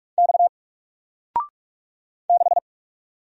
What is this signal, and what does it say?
Morse code, heard as a keyed tone, about 44 wpm, 700 Hz. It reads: X 6
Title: Single Letter-Number 45wpm